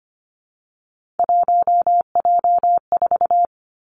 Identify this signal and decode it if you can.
Morse code, heard as a keyed tone, about 25 wpm, 700 Hz. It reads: 1J4
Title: Three Challenging Letter-Number Combinations 25wpm